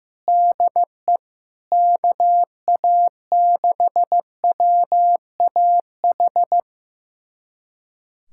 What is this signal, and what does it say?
Morse code, heard as a keyed tone, about 15 wpm, 700 Hz. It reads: DE KA6WAH
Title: USA Callsign DE Callsign - 6 Characters - 15wpm